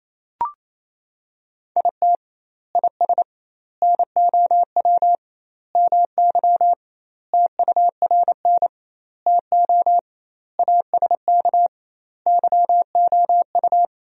Morse code, heard as a keyed tone, about 27 wpm, 700 Hz. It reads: IT IS NOW MY TURN TO ASK YOU
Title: 2x Word-Spacing - Sentences from Top 200 Words - 28wpm